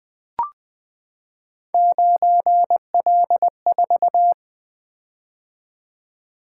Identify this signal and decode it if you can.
Morse code, heard as a keyed tone, about 20 wpm, 700 Hz. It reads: 9L4
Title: Three Letter-Number Combinations - 20wpm